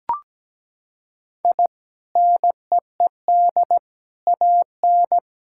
Morse code, heard as a keyed tone, about 17 wpm, 700 Hz. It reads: I NEED AN
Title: Sentences from Top 1000 Words - Review Part 5 - 17wpm